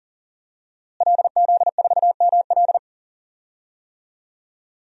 Morse code, heard as a keyed tone, about 40 wpm, 700 Hz. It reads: LZ4ML
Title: DXCC-100 Call Signs - 5 Characters - Repeated 2x - 40wpm